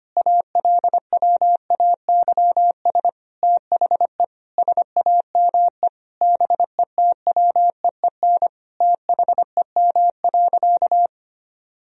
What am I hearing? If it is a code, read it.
Morse code, heard as a keyed tone, about 25 wpm, 700 Hz. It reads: ALWAYS THE SAME BETWEEN THEM.
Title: Sentences from Top 1000 Words - Review Part 1 - 25wpm